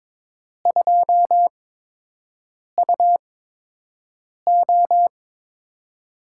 Morse code, heard as a keyed tone, about 22 wpm, spaced as for 6 wpm, 700 Hz. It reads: 2UO
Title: USA Call Signs - 4 Characters - 8x Character Spacing - 22wpm